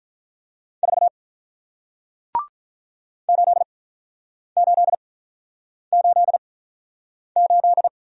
Morse code, heard as a keyed tone, about 46 wpm, spaced as for 9 wpm, 700 Hz. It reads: 4 8888